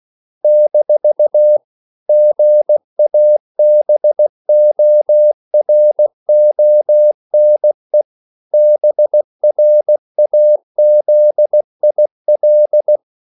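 Morse code, heard as a keyed tone, about 16 wpm, 600 Hz. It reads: - GABORONE BRAZIL
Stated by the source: CW practice of national capitals at 16 words per minute, from ON6ZQ.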